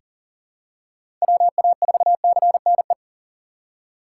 Morse code, heard as a keyed tone, about 40 wpm, 700 Hz. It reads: WA4CNE
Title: USA Call Signs - 3x Character Spacing - 40wpm